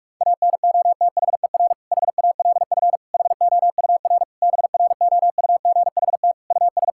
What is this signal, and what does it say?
Morse code, heard as a keyed tone, about 46 wpm, 700 Hz. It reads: ANOTHER HALF HOUR BROUGHT US
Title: Sentences from Top 1600 Words 45wpm